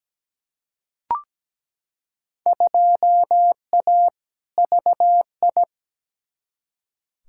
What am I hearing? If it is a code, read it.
Morse code, heard as a keyed tone, about 17 wpm, 700 Hz. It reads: 2A VI
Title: Field Day Exchanges - 17wpm